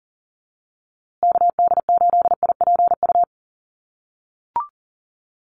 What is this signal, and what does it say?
Morse code, heard as a keyed tone, about 40 wpm, 700 Hz. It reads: KD8IPU
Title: USA Call Signs Repeated 2x 40wpm